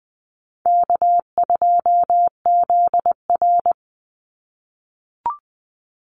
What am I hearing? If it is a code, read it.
Morse code, heard as a keyed tone, about 20 wpm, 700 Hz. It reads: K2ZR
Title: Easy Call Signs - 20wpm